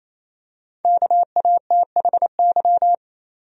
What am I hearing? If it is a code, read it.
Morse code, heard as a keyed tone, about 28 wpm, 700 Hz. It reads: KATHY
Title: USA Common Names - 28wpm